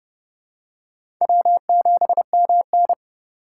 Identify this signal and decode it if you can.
Morse code, heard as a keyed tone, about 31 wpm, 700 Hz. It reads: W7MN
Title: USA Call Signs - 3x Character Spacing - 30wpm